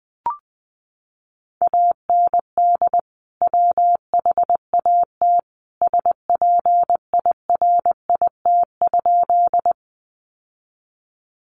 Morse code, heard as a keyed tone, about 20 wpm, 700 Hz. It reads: AND WHAT SPIRIT?
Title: Sentences from Top 1400 Words 20wpm